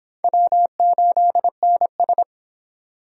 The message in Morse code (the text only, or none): W8NS